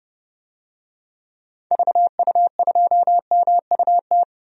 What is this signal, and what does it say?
Morse code, heard as a keyed tone, about 32 wpm, 700 Hz. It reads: VU2MUT
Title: DXCC-100 Call Signs - 6 Characters - Repeated 2x - 30wpm